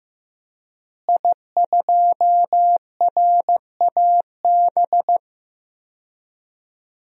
Morse code, heard as a keyed tone, about 15 wpm, 700 Hz. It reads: I2RAB
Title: DXCC-100 Call Signs - 5 Characters - Repeated 3x - 15wpm